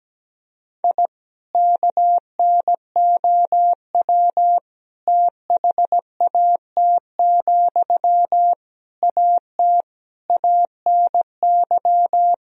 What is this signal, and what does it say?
Morse code, heard as a keyed tone, about 17 wpm, 700 Hz. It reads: I KNOW THAT, AT ANY
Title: Sentences from Top 1000 Words - Review Part 4 - 17wpm